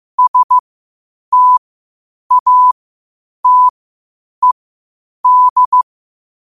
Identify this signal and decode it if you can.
Morse code, heard as a keyed tone, about 15 wpm, spaced as for 8 wpm, 1000 Hz. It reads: STATED